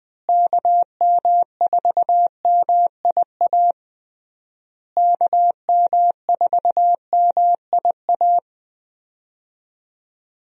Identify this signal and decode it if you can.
Morse code, heard as a keyed tone, about 20 wpm, 700 Hz. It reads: KM4MIA KM4MIA
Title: USA Call Signs - 6 Characters - Repeated 3x - 20wpm